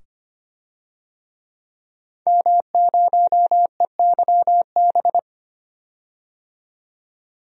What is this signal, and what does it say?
Morse code, heard as a keyed tone, about 25 wpm, 700 Hz. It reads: M0EYB
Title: DXCC-100 Call Signs - 5 Characters - 25wpm